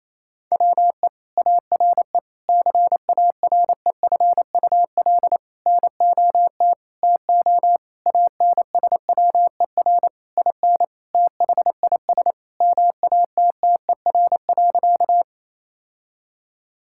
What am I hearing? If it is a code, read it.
Morse code, heard as a keyed tone, about 28 wpm, 700 Hz. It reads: WE ARE CAREFUL NOT TO ANSWER IN THIS MATTER.
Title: Sentences from Top 1600 Words 28wpm